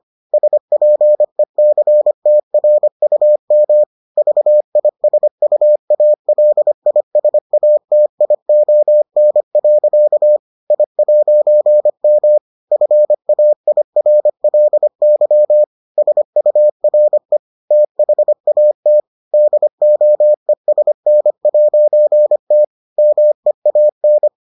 Morse code, heard as a keyed tone, about 25 wpm, 600 Hz. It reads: SPECTRUM VISUALISATION. I'M FAIRLY SURE THAT DOESN'T MEAN